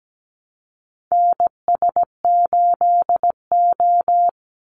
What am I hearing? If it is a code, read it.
Morse code, heard as a keyed tone, about 17 wpm, 700 Hz. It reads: NS8O
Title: Call Signs - 17wpm